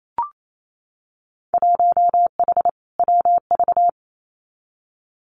Morse code, heard as a keyed tone, about 28 wpm, 700 Hz. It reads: 1H WV
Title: Winter Field Day Exchanges 28wpm